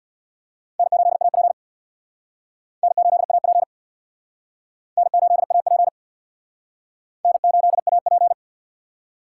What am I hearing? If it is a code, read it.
Morse code, heard as a keyed tone, about 59 wpm, 700 Hz. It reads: N8AP N8AP N8AP N8AP
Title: Speed Racing - USA 1x2 Call Signs 50wpm